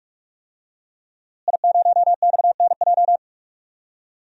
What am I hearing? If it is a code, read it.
Morse code, heard as a keyed tone, about 46 wpm, 700 Hz. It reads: I0XNJ
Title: DXCC-100 Call Signs - 5 Characters - Repeated 2x - 45wpm